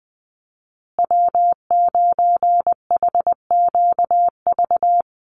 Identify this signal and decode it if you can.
Morse code, heard as a keyed tone, about 20 wpm, 700 Hz. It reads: W9HQV
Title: USA Call Signs Repeated 2x 20wpm